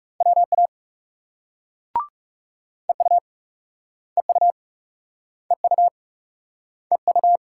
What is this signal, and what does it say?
Morse code, heard as a keyed tone, about 38 wpm, 700 Hz. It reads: WA EU EU EU EU